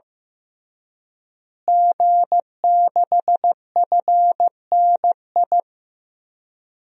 Morse code, heard as a keyed tone, about 15 wpm, 700 Hz. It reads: G6FNI
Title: DXCC-100 Call Signs - 5 Characters - Repeated 3x - 15wpm